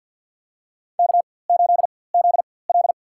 Morse code, heard as a keyed tone, about 52 wpm, spaced as for 23 wpm, 700 Hz. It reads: K8ZL
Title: USA Call Signs - 4x Character Spacing - 50wpm